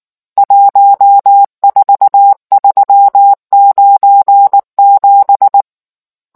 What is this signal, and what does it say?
Morse code, heard as a keyed tone, about 19 wpm, 800 Hz. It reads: 14397